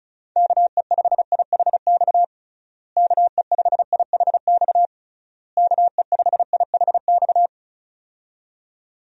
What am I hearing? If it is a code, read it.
Morse code, heard as a keyed tone, about 36 wpm, 700 Hz. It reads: KE5IHX KE5IHX KE5IHX
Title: USA Call Signs - 6 Characters - Repeated 3x - 35wpm